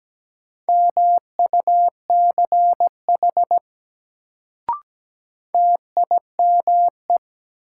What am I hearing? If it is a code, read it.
Morse code, heard as a keyed tone, about 17 wpm, 700 Hz. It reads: MUCH TIME
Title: Sets of 2 Words - New Word Drill - 17wpm